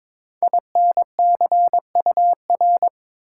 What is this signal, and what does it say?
Morse code, heard as a keyed tone, about 23 wpm, 700 Hz. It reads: INCUR
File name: 5 Letter Words - First 3 Letters Given - 22wpm